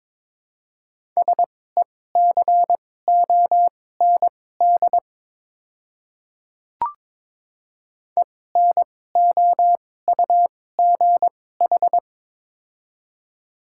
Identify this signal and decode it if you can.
Morse code, heard as a keyed tone, about 22 wpm, spaced as for 16 wpm, 700 Hz. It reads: SECOND ENOUGH
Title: 2x Character Spacing - Top 500 Words - 22wpm